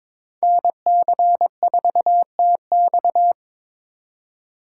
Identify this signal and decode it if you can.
Morse code, heard as a keyed tone, about 26 wpm, 700 Hz. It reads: NC4TX
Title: Speed Racing - USA 2x2 Call Signs 22wpm